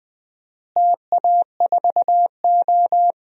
Morse code, heard as a keyed tone, about 20 wpm, 700 Hz. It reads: TA4O
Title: DXCC-100 Call Signs - 4 Characters - Repeated 3x - 20wpm